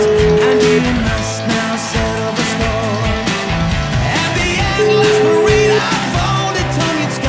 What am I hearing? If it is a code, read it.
Ringback tone (Europe, 425 Hz), with noise.